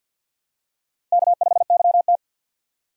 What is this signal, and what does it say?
Morse code, heard as a keyed tone, about 52 wpm, 700 Hz. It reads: K5YT
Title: USA Call Signs - 4 Characters - Repeated 2x - 50wpm